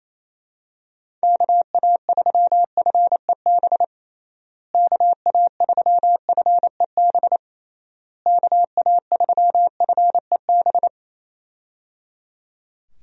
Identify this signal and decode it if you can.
Morse code, heard as a keyed tone, about 27 wpm, 700 Hz. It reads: KA3FEB KA3FEB KA3FEB
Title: USA Call Signs - 6 Characters - Repeated 3x - 28wpm